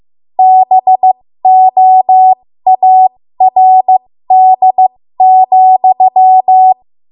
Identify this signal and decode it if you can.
Morse code, heard as a keyed tone, about 15 wpm, spaced as for 13 wpm, 750 Hz. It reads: BOARD,
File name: Morse / 13WPM